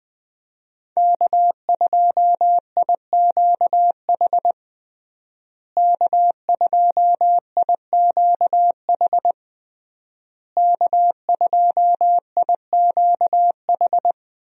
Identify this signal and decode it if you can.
Morse code, heard as a keyed tone, about 20 wpm, 700 Hz. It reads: K2IQH K2IQH K2IQH
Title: USA Call Signs - 5 Characters - Repeated 3x - 20wpm